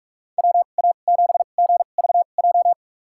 Morse code, heard as a keyed tone, about 44 wpm, spaced as for 32 wpm, 700 Hz. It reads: WA7GVJ
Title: USA Call Signs - 2x Character Spacing - 45wpm